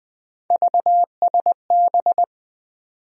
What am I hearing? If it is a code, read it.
Morse code, heard as a keyed tone, about 20 wpm, 700 Hz. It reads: VSB
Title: USA Call Sign Postfixes - 3 Letters 20wpm